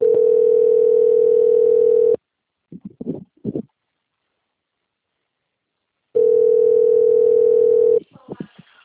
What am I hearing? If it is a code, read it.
Ringback tone (North America), 440 + 480 Hz.